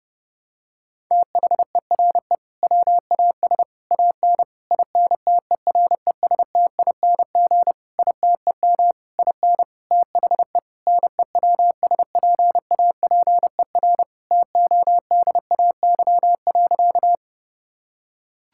Morse code, heard as a keyed tone, about 30 wpm, 700 Hz. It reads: THERE WAS AN INTERESTING ITEM IN THE NEWSPAPER TODAY.
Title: Sentences from Top 1000 Words - Review Part 6 - 30wpm